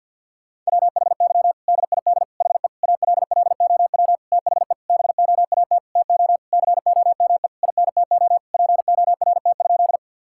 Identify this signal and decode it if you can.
Morse code, heard as a keyed tone, about 52 wpm, 700 Hz. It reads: WHY DID HE ALLOW THE BOAT TO COME INTO PORT?
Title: Sentences from Top 1000 Words - Review Part 5 - 50wpm